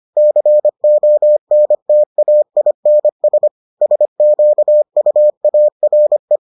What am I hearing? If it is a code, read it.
Morse code, heard as a keyed tone, about 25 wpm, 600 Hz. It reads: CONTAINS SQUARE